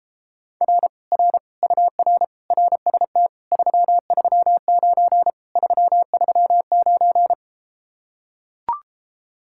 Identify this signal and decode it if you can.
Morse code, heard as a keyed tone, about 33 wpm, 700 Hz. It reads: R R UR RST 339 339